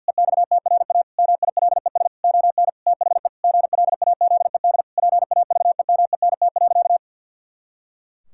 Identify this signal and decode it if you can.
Morse code, heard as a keyed tone, about 50 wpm, 700 Hz. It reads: EXTRA MILES ON THE GLAZED PAVEMENT.
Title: Pangram Challenge v2 - 50wpm